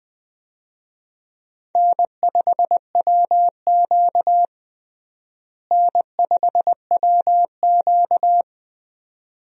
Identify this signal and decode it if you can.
Morse code, heard as a keyed tone, about 21 wpm, 700 Hz. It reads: N5WQ N5WQ
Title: USA Call Signs - 4 Characters - Repeated 2x - 20wpm